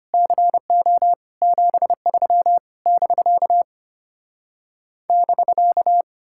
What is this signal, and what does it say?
Morse code, heard as a keyed tone, about 33 wpm, 700 Hz. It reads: CO 73 <BK> <BK>